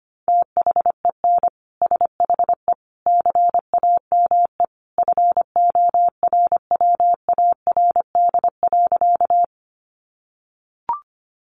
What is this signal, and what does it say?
Morse code, heard as a keyed tone, about 25 wpm, 700 Hz. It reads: THEN SHE CAME FORWARD.
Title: Sentences from Top 1300 Words 25wpm